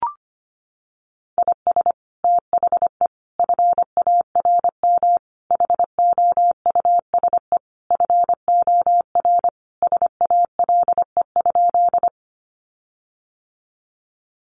Morse code, heard as a keyed tone, about 25 wpm, 700 Hz. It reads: IS THE FARM HOUSE FOR SALE?
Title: Sentences from Top 1400 Words 25wpm